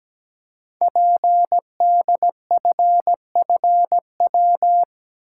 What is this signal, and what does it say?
Morse code, heard as a keyed tone, about 17 wpm, 700 Hz. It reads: PDFFW
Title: Code Groups - 5 Letter:Numbers - 17wpm